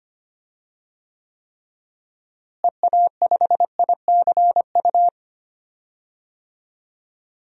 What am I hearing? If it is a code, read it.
Morse code, heard as a keyed tone, about 25 wpm, 700 Hz. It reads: EA5ICU
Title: DXCC-100 Call Signs - 6 Characters - Repeated 3x - 25wpm